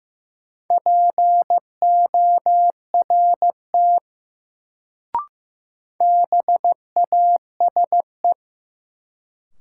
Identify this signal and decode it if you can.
Morse code, heard as a keyed tone, about 15 wpm, 700 Hz. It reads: PORT BASE
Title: Sentences from Top 1000 Words - Review Part 1 - New Word Drill - 15wpm